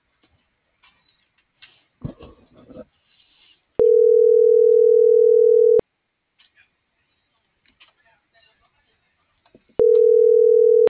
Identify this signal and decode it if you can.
Ringback tone (North America), 440 + 480 Hz.